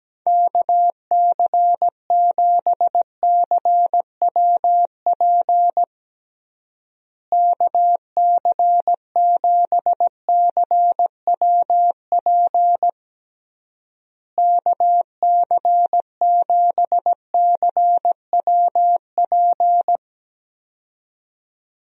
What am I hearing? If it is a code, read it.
Morse code, heard as a keyed tone, about 17 wpm, 700 Hz. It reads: KC7CWP KC7CWP KC7CWP
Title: USA Call Signs - 6 Characters - Repeated 3x - 17wpm